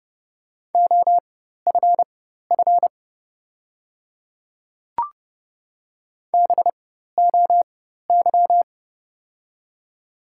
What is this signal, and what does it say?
Morse code, heard as a keyed tone, about 30 wpm, spaced as for 14 wpm, 700 Hz. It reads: OFF BOY
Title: 4x Character Spacing - Top 200 Words - 30wpm